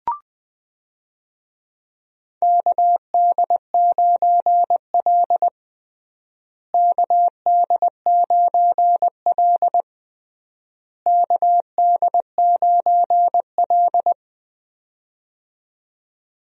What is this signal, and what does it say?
Morse code, heard as a keyed tone, about 21 wpm, 700 Hz. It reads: KD9L KD9L KD9L
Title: USA Call Signs - 4 Characters - Repeated 3x - 20wpm